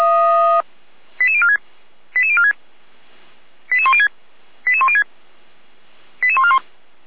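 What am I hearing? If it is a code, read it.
Five-tone selective call (ZVEI), heard as quick runs of five tones, tones of about 70 ms. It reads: 88046 88046 88106 88106 88141
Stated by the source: Received on an SDR from server audio.f85360.feuerwehr-orlamuende.org (Feuerwehr Orlamünde fire brigade).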